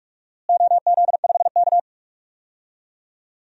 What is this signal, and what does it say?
Morse code, heard as a keyed tone, about 43 wpm, 700 Hz. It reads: OZ5K
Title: DXCC-100 Call Signs - 4 Characters - Repeated 3x - 45wpm